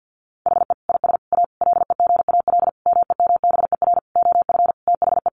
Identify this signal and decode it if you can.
Morse code, heard as a keyed tone, about 50 wpm, 700 Hz. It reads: HE IS A LEGAL MEMBER OF THE
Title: Sentences from Top 1200 Words 50wpm